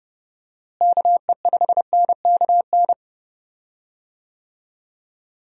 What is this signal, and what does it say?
Morse code, heard as a keyed tone, about 30 wpm, 700 Hz. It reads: KE5NKN
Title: USA Call Signs - 6 Characters - Repeated 3x - 30wpm